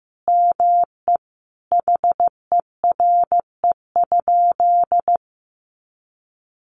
Morse code, heard as a keyed tone, about 15 wpm, 700 Hz. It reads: ME HERE?
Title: Sentences from Top 1200 Words 15wpm